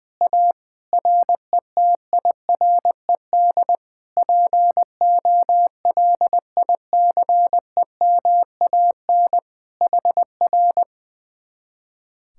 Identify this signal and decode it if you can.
Morse code, heard as a keyed tone, about 20 wpm, 700 Hz. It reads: A RETIRED POLICEMAN HR
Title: Speed-Racing Common QSO Phrases - 20wpm